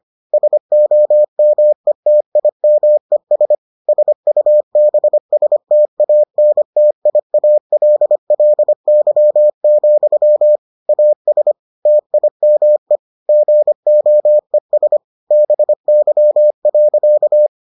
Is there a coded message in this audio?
Yes: Morse code from 0.3 to 17.5 s, heard as a keyed tone, about 25 wpm, 600 Hz. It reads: SOMETIMES SUBSTANTIALLY, AS TIME GOES BY.